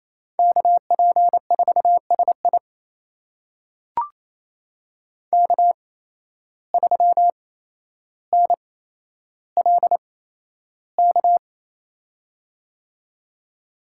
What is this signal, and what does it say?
Morse code, heard as a keyed tone, about 28 wpm, spaced as for 12 wpm, 700 Hz. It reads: KP4SI K3NLK
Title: USA Call Signs - 5 Characters - 8x Character Spacing - 28wpm